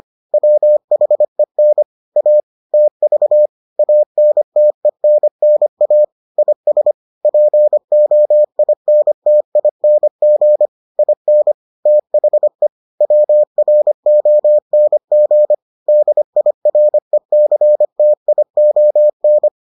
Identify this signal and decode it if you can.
Morse code, heard as a keyed tone, about 25 wpm, 600 Hz. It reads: WHEN A TV ANTENNA IS POINTING IN THE WRONG DIRECTION